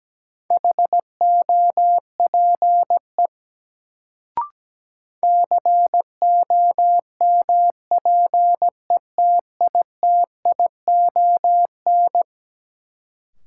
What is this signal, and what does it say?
Morse code, heard as a keyed tone, about 17 wpm, 700 Hz. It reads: HOPE COMPETITION
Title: Sentences from Top 1000 Words - New Word Drill - 17wpm